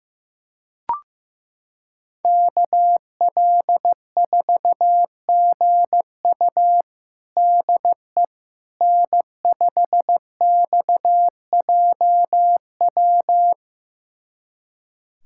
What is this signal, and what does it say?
Morse code, heard as a keyed tone, about 15 wpm, 700 Hz. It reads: KL4GU DE N5XJW
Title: USA Callsign DE Callsign - 5 Characters - 15wpm